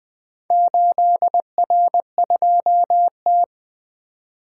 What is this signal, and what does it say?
Morse code, heard as a keyed tone, about 20 wpm, 700 Hz. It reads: 8R2T
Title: Four Letter-Number Combinations - 20wpm